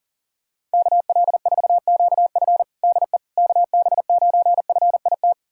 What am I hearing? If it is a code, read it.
Morse code, heard as a keyed tone, about 40 wpm, 700 Hz. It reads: KL4QF DE KB9FIT
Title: USA Callsign DE Callsign 40wpm